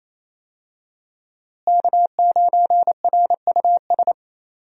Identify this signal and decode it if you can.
Morse code, heard as a keyed tone, about 28 wpm, 700 Hz. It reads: K9RUS
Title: USA Call Signs - 5 Characters - Repeated 3x - 28wpm